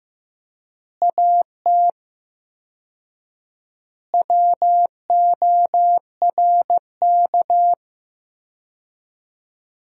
Morse code, heard as a keyed tone, about 15 wpm, 700 Hz. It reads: AT WORK
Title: 4x Word-Spacing - Sentences from Top 200 Words - 15wpm